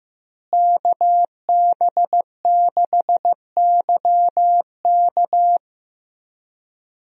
Morse code, heard as a keyed tone, about 15 wpm, 700 Hz. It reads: KB6YK
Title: USA Call Signs - 5 Characters - Repeated 3x - 15wpm